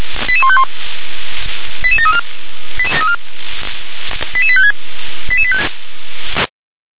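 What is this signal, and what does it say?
Five-tone selective call (ZVEI), heard as quick runs of five tones, tones of about 70 ms. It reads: 88151 88734 88856
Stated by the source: ZVEI five-tone selective call received with an SDR.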